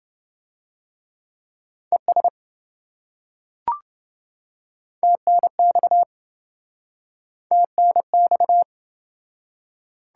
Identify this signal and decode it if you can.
Morse code, heard as a keyed tone, about 25 wpm, 700 Hz. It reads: ES TNX TNX